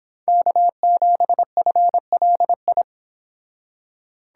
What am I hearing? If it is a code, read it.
Morse code, heard as a keyed tone, about 29 wpm, 700 Hz. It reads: K7FLI